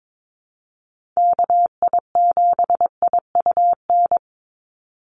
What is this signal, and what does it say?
Morse code, heard as a keyed tone, about 22 wpm, 700 Hz. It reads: KI7IUN
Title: USA Call Signs 22wpm